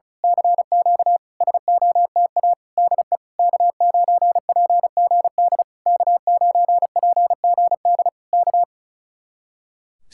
Morse code, heard as a keyed tone, about 33 wpm, 700 Hz. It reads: CQ SOTA DE K9PGD K9PGD K
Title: Speed-Racing Common QSO Phrases Encore - 35wpm